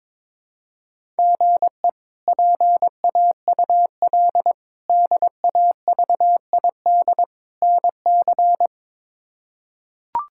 Morse code, heard as a keyed tone, about 22 wpm, 700 Hz. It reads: GE PAUL DAVID NC
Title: K1USN SST QSO Practice - 22wpm